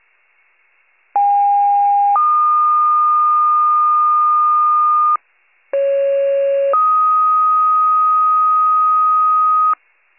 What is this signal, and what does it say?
Two-tone page, heard as a one-second tone, then a three-second tone, about 810 then 1220 Hz, about 550 then 1250 Hz. It is heard from 1.2 to 9.7 s.
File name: wcalert